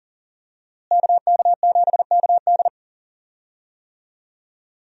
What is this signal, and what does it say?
Morse code, heard as a keyed tone, about 40 wpm, 700 Hz. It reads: KK7KD
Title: USA Call Signs - 5 Characters - Repeated 3x - 40wpm